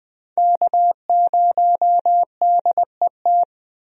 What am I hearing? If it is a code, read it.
Morse code, heard as a keyed tone, about 20 wpm, 700 Hz. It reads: K0DET
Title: USA Call Signs - 5 Characters - Repeated 3x - 20wpm